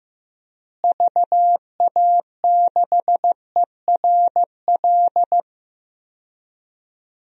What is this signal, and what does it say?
Morse code, heard as a keyed tone, about 15 wpm, 700 Hz. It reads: VA6ERL
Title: DXCC-100 Call Signs - 6 Characters - Repeated 2x - 15wpm